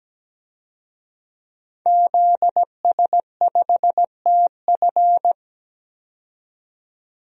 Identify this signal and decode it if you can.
Morse code, heard as a keyed tone, about 17 wpm, 700 Hz. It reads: ZS5TF
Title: DXCC-100 Call Signs - 5 Characters - 17wpm